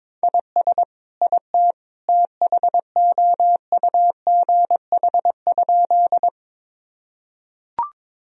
Morse code, heard as a keyed tone, about 22 wpm, 700 Hz. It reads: IS IT THOUGH?
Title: Sentences from Top 1000 Words - Review Part 2 - 22wpm